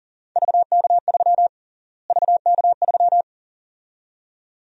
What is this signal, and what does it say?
Morse code, heard as a keyed tone, about 40 wpm, 700 Hz. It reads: VK3 VK3
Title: USA Call Sign Prefixes - Repeated 2x - 40wpm